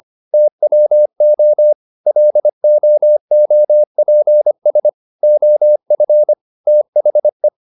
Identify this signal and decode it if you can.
Morse code, heard as a keyed tone, about 25 wpm, 600 Hz. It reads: TWO LOOPS OF THE